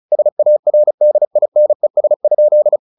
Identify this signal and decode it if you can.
Morse code, heard as a keyed tone, about 34 wpm, 600 Hz. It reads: SARDINES?
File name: the old man and the sea / the old man and the sea 1 35wpm